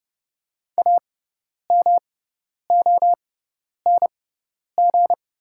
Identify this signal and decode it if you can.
Morse code, heard as a keyed tone, about 30 wpm, spaced as for 10 wpm, 690 Hz. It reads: AMONG
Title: 6x Character Spacing - Top 500 Words - 30wpm